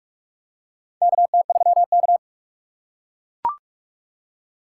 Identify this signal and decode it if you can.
Morse code, heard as a keyed tone, about 44 wpm, 700 Hz. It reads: KT3K
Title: USA Call Signs - 4 Characters - 4x Character Spacing - 45wpm